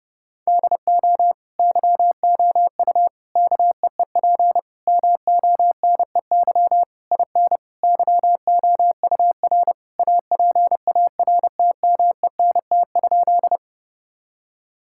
Morse code, heard as a keyed tone, about 30 wpm, 700 Hz. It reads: DO YOU KEEP MONEY IN YOUR APARTMENT?
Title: Sentences from Top 1000 Words - Review Part 6 - 30wpm